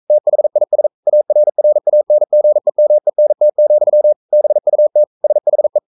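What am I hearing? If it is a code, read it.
Morse code, heard as a keyed tone, about 42 wpm, 600 Hz. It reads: THIS ARRANGEMENT, BUT SHE
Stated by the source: Morse code sending of an English novel at 42 wpm.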